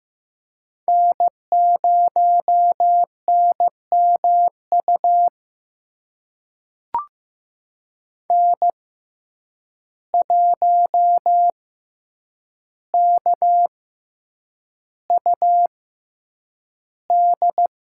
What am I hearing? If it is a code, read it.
Morse code, heard as a keyed tone, about 15 wpm, spaced as for 8 wpm, 700 Hz. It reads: N0NMU N1KUD
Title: USA Call Signs - 5 Characters - 6x Character Spacing - 15wpm